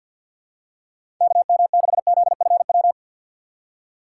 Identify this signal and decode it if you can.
Morse code, heard as a keyed tone, about 46 wpm, 700 Hz. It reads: KM6ZFW
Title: USA Call Signs - 6 Characters - 4x Character Spacing - 50wpm